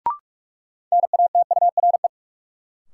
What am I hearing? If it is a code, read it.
Morse code, heard as a keyed tone, about 45 wpm, 700 Hz. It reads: NATURE
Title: Sentences from Top 600 Words - New Word Drill - 45wpm